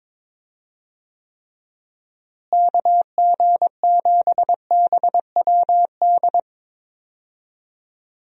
Morse code, heard as a keyed tone, about 21 wpm, 700 Hz. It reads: KG7BWD
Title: USA Call Signs - 6 Characters - Repeated 3x - 22wpm